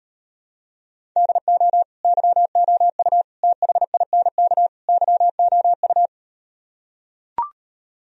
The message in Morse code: DO YOU THINK YOU

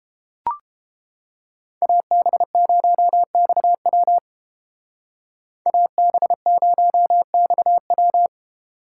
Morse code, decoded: AB0XW AB0XW

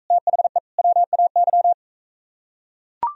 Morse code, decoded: THE WAY

